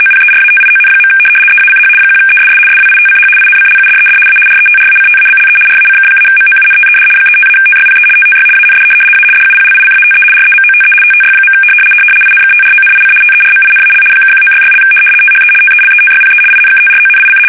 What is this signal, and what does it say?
RTTY, about 75 baud. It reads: OPERATION. EACH 1ESS SWITCH INCLUDED A RETRACTABLE RACK-MOUNTED 35-TYPE TTY AND A DUPLICATE REMOTE TTY. INKTRONIC TERMINAL (1966) THE TELETYPE INKTRONIC
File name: fsk850-75B / teletype-corp